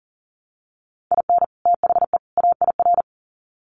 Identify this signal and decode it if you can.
Morse code, heard as a keyed tone, about 40 wpm, 700 Hz. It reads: IN THE AIR
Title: Sets of 3 Words 40wpm